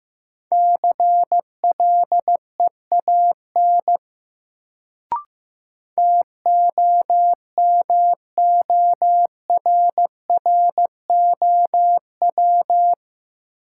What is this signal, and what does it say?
Morse code, heard as a keyed tone, about 16 wpm, 700 Hz. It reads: CLEAN TOMORROW
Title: Sentences from Top 1100 Words - New Word Drill - 15wpm